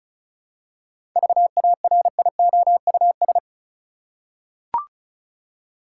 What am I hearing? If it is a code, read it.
Morse code, heard as a keyed tone, about 34 wpm, 700 Hz. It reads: VARIOUS
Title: Seven Letter Words - 35wpm